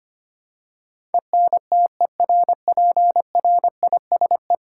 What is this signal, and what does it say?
Morse code, heard as a keyed tone, about 25 wpm, 700 Hz. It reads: ENTERPRISE
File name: 10 Letter Words - First 3 Letters Given - 25wpm